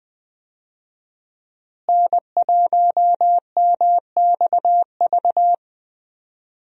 Morse code, heard as a keyed tone, about 20 wpm, 700 Hz. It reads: N1MXV
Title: USA Call Signs - 5 Characters - Repeated 3x - 20wpm